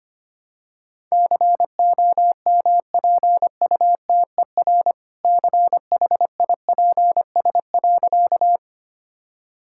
Morse code, heard as a keyed tone, about 25 wpm, 700 Hz. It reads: COMPUTER CHIPS.
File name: practice / 25wpm / All-Age Appropriate Jokes - 25wpm